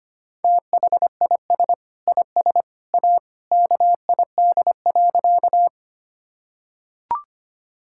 Morse code, heard as a keyed tone, about 25 wpm, 700 Hz. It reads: THIS IS A KID.
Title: Sentences from Top 1600 Words 25wpm